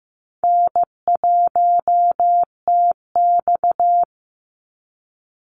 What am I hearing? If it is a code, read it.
Morse code, heard as a keyed tone, about 15 wpm, 700 Hz. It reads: N1TX
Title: Call Signs Repeated 3x - 15wpm